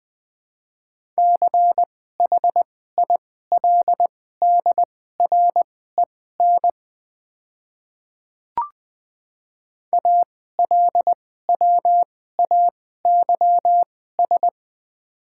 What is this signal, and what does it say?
Morse code, heard as a keyed tone, about 20 wpm, spaced as for 14 wpm, 700 Hz. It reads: CHILDREN ALWAYS
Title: 2x Character Spacing - Top 300 Words - 20wpm